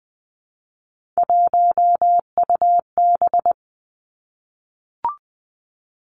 Morse code, heard as a keyed tone, about 19 wpm, 700 Hz. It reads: 1UB
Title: Three Challenging Letter-Number Combinations 20wpm